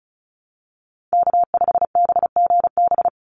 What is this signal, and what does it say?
Morse code, heard as a keyed tone, about 34 wpm, 700 Hz. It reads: K5BGB
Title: Call Signs - 35wpm